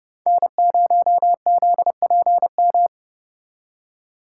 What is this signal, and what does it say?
Morse code, heard as a keyed tone, about 38 wpm, 700 Hz. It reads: N0ZPM